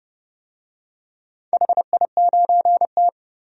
Morse code, heard as a keyed tone, about 30 wpm, 700 Hz. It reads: HI9T
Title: DXCC-100 Call Signs - 4 Characters - 30wpm